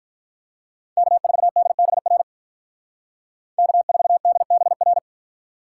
Morse code, heard as a keyed tone, about 43 wpm, 700 Hz. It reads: K4DBR K4DBR
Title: Speed Racing - USA 1x3 Call Signs 35wpm